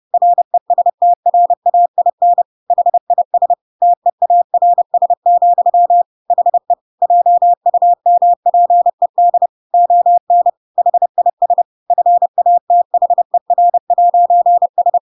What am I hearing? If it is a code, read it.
Morse code, heard as a keyed tone, about 30 wpm, 700 Hz. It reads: RESTRAIN HIS TEARS, HE JUMPED ON HIS FATHER'S
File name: Pinocchio / Pinocchio ch8 30 30 ik1hge